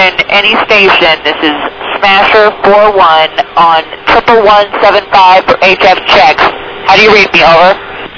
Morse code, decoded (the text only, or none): none